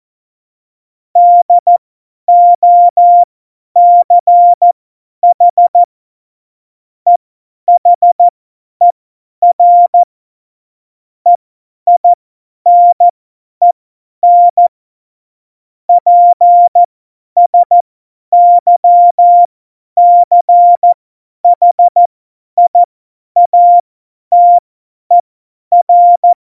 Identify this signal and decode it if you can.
Morse code, heard as a keyed tone, about 14 wpm, spaced as for 10 wpm, 700 Hz. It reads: DOCH EHER EINEN PSYCHIATER